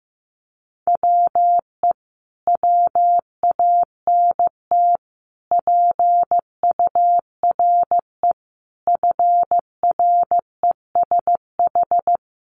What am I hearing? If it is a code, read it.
Morse code, heard as a keyed tone, about 14 wpm, 700 Hz. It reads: WE WANT PURE FRESH
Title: Sentences from Top 1500 Words 15wpm